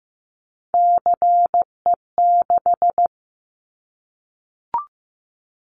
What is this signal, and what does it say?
Morse code, heard as a keyed tone, about 15 wpm, 700 Hz. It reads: CE6
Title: Callsign Prefixes for Top 100 DXCC - 15wpm